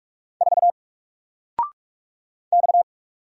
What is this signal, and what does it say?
Morse code, heard as a keyed tone, about 45 wpm, 700 Hz. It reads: 4 X
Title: Single Letter-Number 45wpm